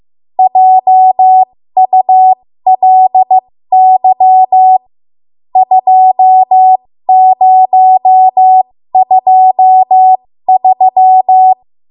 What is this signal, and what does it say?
Morse code, heard as a keyed tone, about 15 wpm, spaced as for 13 wpm, 750 Hz. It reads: JULY 2023
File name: Morse / 13WPM